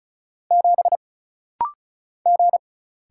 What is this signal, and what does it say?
Morse code, heard as a keyed tone, about 36 wpm, 700 Hz. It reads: 7 G
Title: Single Letter-Number - Rapid-Fire 35wpm